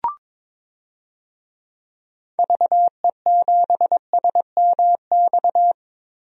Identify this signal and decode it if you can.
Morse code, heard as a keyed tone, about 22 wpm, 700 Hz. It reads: VE7SMX
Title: DXCC-100 Call Signs - 6 Characters - Repeated 2x - 22wpm